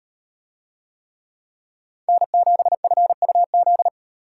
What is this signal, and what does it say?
Morse code, heard as a keyed tone, about 35 wpm, 700 Hz. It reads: N7FUZ